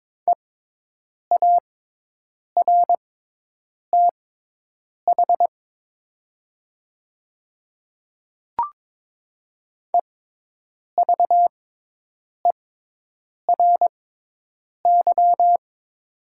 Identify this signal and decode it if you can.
Morse code, heard as a keyed tone, about 21 wpm, spaced as for 8 wpm, 700 Hz. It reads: EARTH EVERY
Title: 6x Character Spacing - Top 300 Words - 22wpm